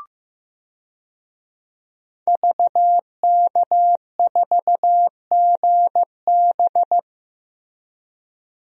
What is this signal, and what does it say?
Morse code, heard as a keyed tone, about 15 wpm, 700 Hz. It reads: VK4GB
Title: DXCC-100 Call Signs - 5 Characters - Repeated 2x - 15wpm